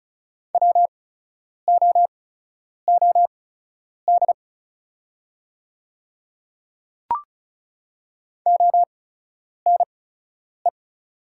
Morse code, heard as a keyed tone, about 36 wpm, spaced as for 10 wpm, 700 Hz. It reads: WOOD ONE
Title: 8x Character Spacing - Top 300 Words - 35wpm